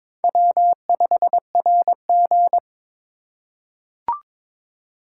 Morse code, heard as a keyed tone, about 22 wpm, 700 Hz. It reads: W5RG
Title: USA Call Signs - 2x Character Spacing - 22wpm